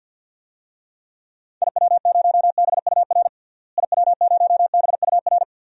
Morse code, heard as a keyed tone, about 49 wpm, 700 Hz. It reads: IW0BUR IW0BUR
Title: DXCC-100 Call Signs - 6 Characters - Repeated 3x - 50wpm